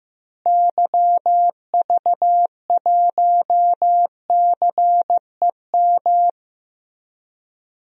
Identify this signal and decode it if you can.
Morse code, heard as a keyed tone, about 16 wpm, 700 Hz. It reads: YV1CEM
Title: DXCC-100 Call Signs - 6 Characters - Repeated 3x - 15wpm